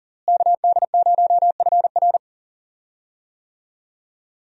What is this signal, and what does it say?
Morse code, heard as a keyed tone, about 40 wpm, 700 Hz. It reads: KD0FR
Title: USA Call Signs - 5 Characters - Repeated 3x - 40wpm